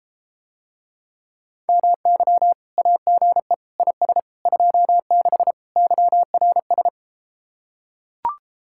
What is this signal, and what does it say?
Morse code, heard as a keyed tone, about 33 wpm, 700 Hz. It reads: MY AGE IS 26 YRS